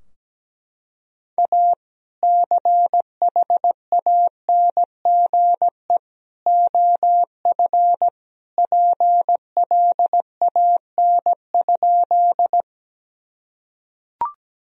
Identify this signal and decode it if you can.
Morse code, heard as a keyed tone, about 17 wpm, 700 Hz. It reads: A CHANGE OF PLAN?
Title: Sentences from Top 1000 Words - Review Part 2 - 17wpm